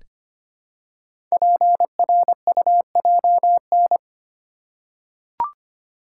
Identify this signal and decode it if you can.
Morse code, heard as a keyed tone, about 25 wpm, 700 Hz. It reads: PRUJN
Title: Code Groups - 5 Letters - 25wpm